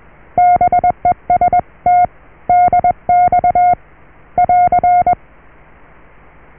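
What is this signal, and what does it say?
Morse code, heard as a keyed tone, about 21 wpm, 720 Hz. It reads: BEST DX <AR>
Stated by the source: Morse QSO on 14 MHz with a German station, received at RST 579.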